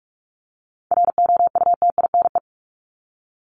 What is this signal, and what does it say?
Morse code, heard as a keyed tone, about 44 wpm, 700 Hz. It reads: ROUTINE
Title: Top 1501-1750 Words 45wpm